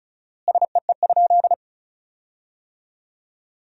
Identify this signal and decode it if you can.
Morse code, heard as a keyed tone, about 36 wpm, 700 Hz. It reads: SEE?
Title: 4x Word-Spacing - Sentences from Top 100 Words - 35wpm